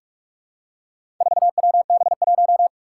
Morse code, heard as a keyed tone, about 44 wpm, 700 Hz. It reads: 4WB1
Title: Four Challenging Letter-Number Combinations 45wpm